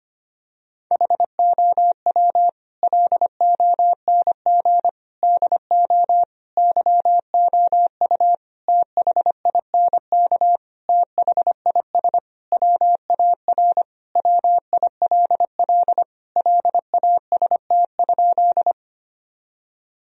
Morse code, heard as a keyed tone, about 25 wpm, 700 Hz. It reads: HOW LONG DO YOU THINK THIS WAR WILL LAST?
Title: Sentences from Top 1000 Words - Review Part 2 - 25wpm